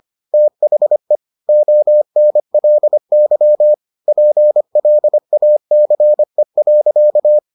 Morse code, heard as a keyed tone, about 25 wpm, 600 Hz. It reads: THE ONLY PLACE.